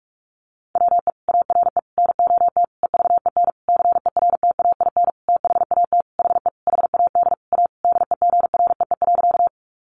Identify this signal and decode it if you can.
Morse code, heard as a keyed tone, about 44 wpm, 700 Hz. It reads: WE ARE NOT EVEN CERTAIN THAT HE HAD A DEGREE.
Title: Sentences from Top 1500 Words 45wpm